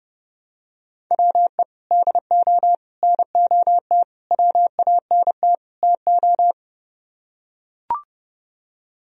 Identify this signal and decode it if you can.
Morse code, heard as a keyed tone, about 30 wpm, 700 Hz. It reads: WE DO NOT WANT TO